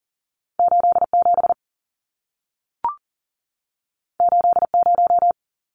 Morse code, heard as a keyed tone, about 40 wpm, 700 Hz. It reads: 87 80
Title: Numbers - 2-Digit 40wpm